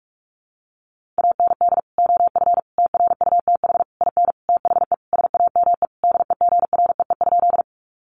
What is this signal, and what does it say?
Morse code, heard as a keyed tone, about 44 wpm, 700 Hz. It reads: AND OF TRUTH IN THE SAME DEGREE?
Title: Sentences from Top 1500 Words 45wpm